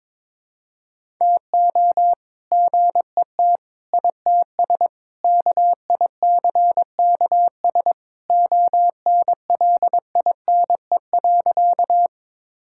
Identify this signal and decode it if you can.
Morse code, heard as a keyed tone, about 22 wpm, 700 Hz. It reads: TO GET ITS KICKS ONLINE.
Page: morse-code-ninja-practice-22wpm